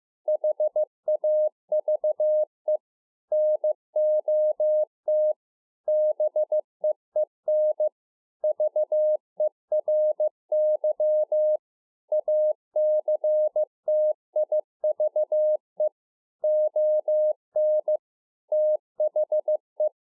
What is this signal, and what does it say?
Morse code, heard as a keyed tone, about 15 wpm, 610 Hz. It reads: HAVE NOT BEEN VERY ACTIVE ON THE
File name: 15WPM